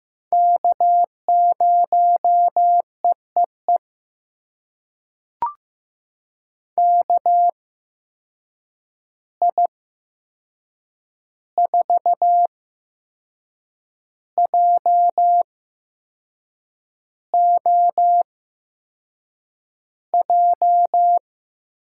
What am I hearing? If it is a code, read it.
Morse code, heard as a keyed tone, about 15 wpm, spaced as for 6 wpm, 700 Hz. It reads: K0EEE KI4JOJ
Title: USA Call Signs - 8x Character Spacing - 15wpm